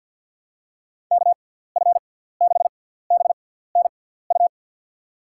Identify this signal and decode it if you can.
Morse code, heard as a keyed tone, about 53 wpm, spaced as for 17 wpm, 700 Hz. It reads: KF6BNU
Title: USA Call Signs - 6 Characters - 6x Character Spacing - 50wpm